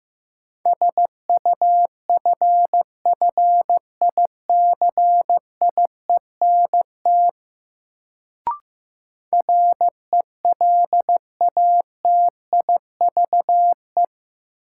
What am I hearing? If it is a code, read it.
Morse code, heard as a keyed tone, about 15 wpm, 700 Hz. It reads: SUFFICIENT RELATIVE
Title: Sentences from Top 1400 Words - New Word Drill - 15wpm